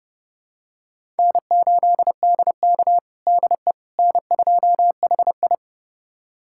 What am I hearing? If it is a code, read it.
Morse code, heard as a keyed tone, about 30 wpm, 700 Hz. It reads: N8DK DE N2HI
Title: USA Callsign DE Callsign - 4 Characters - 30wpm